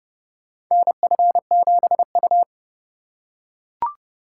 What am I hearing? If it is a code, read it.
Morse code, heard as a keyed tone, about 29 wpm, 700 Hz. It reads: NF7U
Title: USA Call Signs - 4 Characters - 2x Character Spacing - 30wpm